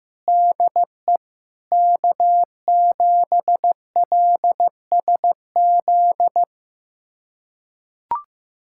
Morse code, heard as a keyed tone, about 16 wpm, 700 Hz. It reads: DE K7LSZ
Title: USA Callsign DE Callsign - 5 Characters - 15wpm